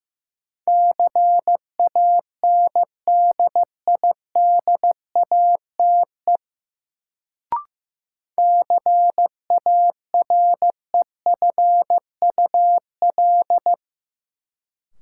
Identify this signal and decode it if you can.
Morse code, heard as a keyed tone, about 15 wpm, 700 Hz. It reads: CANDIDATE CAREFUL
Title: Sentences from Top 1600 Words - New Word Drill - 15wpm